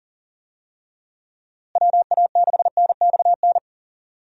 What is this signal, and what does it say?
Morse code, heard as a keyed tone, about 40 wpm, 700 Hz. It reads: WA6NXN
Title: USA Call Signs - 6 Characters - Repeated 2x - 40wpm